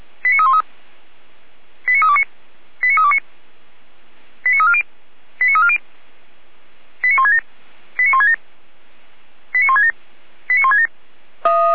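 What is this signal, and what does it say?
Five-tone selective call (ZVEI), heard as quick runs of five tones, tones of about 70 ms. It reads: 89313 89329 89329 89340 89340 89167 89167 89167 89167
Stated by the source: Five-tone ZVEI selective call received on an SDR by feuerwehr-orlamuende, fire brigade alerting.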